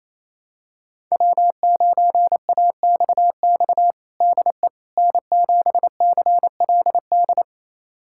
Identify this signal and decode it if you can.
Morse code, heard as a keyed tone, about 28 wpm, 700 Hz. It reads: W9AXX DE N7CLD
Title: USA Callsign DE Callsign - 5 Characters - 28wpm